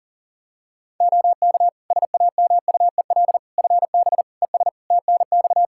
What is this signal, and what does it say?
Morse code, heard as a keyed tone, about 40 wpm, 700 Hz. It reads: OK SAMUEL FB ES TNX
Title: Practice QSO - Exchange 2 - Info - 40wpm